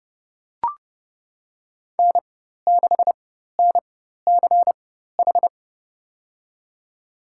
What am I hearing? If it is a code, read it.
Morse code, heard as a keyed tone, about 30 wpm, spaced as for 14 wpm, 690 Hz. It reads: N6NCH
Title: USA Call Signs - 4x Character Spacing - 30wpm